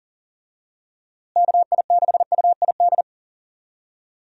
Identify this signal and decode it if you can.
Morse code, heard as a keyed tone, about 40 wpm, 700 Hz. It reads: KI6UID
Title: USA Call Signs - 6 Characters - Repeated 2x - 40wpm